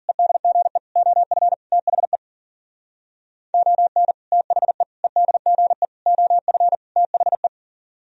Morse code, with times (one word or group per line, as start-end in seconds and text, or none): 0.1-0.8 EDGE
1.0-1.5 OF
1.7-2.2 THE
3.5-4.1 ON
4.3-4.8 THE
5.0-5.9 EDGE
6.1-6.8 OF
7.0-7.5 THE